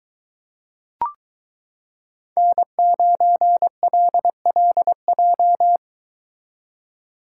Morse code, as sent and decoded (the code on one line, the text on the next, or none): -. ----. .-.. .-.. .---
N9LLJ